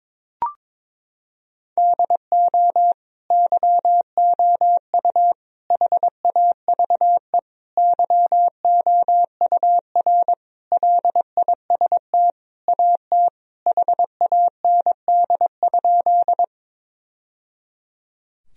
Morse code, heard as a keyed tone, about 22 wpm, 700 Hz. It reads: DO YOU HAVE YOUR LIST AT HAND?
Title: Sentences from Top 1000 Words - Review Part 2 - 22wpm